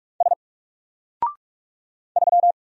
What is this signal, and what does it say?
Morse code, heard as a keyed tone, about 46 wpm, 700 Hz. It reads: S 3
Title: Single Letter-Number Alt 45wpm